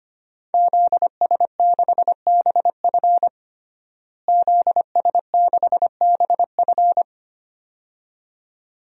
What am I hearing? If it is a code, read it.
Morse code, heard as a keyed tone, about 25 wpm, 700 Hz. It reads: ZS6BF ZS6BF
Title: DXCC-100 Call Signs - 5 Characters - Repeated 2x - 25wpm